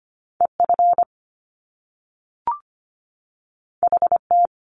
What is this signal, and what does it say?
Morse code, heard as a keyed tone, about 25 wpm, 700 Hz. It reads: EF HT
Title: Top Two Letter Combinations in Words 25wpm